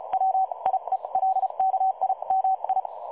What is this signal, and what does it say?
Morse code, heard as a keyed tone, about 33 wpm, spaced as for 28 wpm, 760 Hz. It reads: WIELKIMI
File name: dlazaawansowanych / nagraniazpasma / websdr recording 3565.0kHz